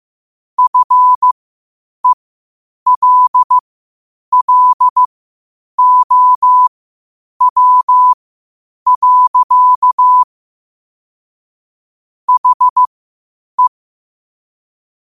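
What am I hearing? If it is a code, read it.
Morse code, heard as a keyed tone, about 15 wpm, spaced as for 8 wpm, 1000 Hz. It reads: FELLOW. HE
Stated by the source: Morse code quotes at 8 WPM.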